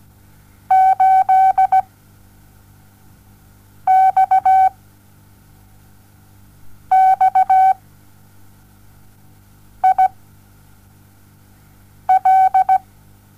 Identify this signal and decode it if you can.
Morse code, heard as a keyed tone, about 17 wpm, spaced as for 4 wpm, 760 Hz. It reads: 8XXIL
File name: Onlinelehrgang / morsen / DJ4UF Morsen07